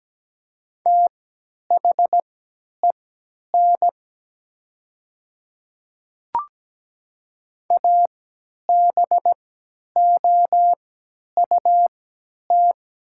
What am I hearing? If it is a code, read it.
Morse code, heard as a keyed tone, about 17 wpm, spaced as for 10 wpm, 700 Hz. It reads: THEN ABOUT
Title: 3x Character Spacing - Top 200 Words - 17wpm